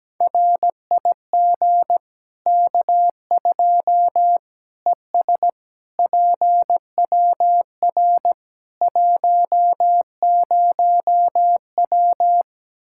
Morse code, heard as a keyed tone, about 17 wpm, 700 Hz. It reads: RIG K2 ES PWR 10W
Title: Practice QSO - Exchange 2 - Info - 17wpm